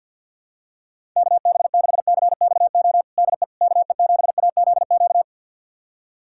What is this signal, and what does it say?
Morse code, heard as a keyed tone, about 51 wpm, 700 Hz. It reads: KB6CXO DE KE7AZQ
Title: USA Callsign DE Callsign - 6 Characters - 50wpm